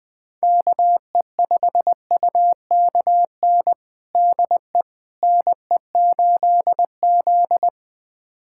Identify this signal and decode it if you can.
Morse code, heard as a keyed tone, about 20 wpm, 700 Hz. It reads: KE5UKN DE NE8Z
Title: USA Callsign DE Callsign 20wpm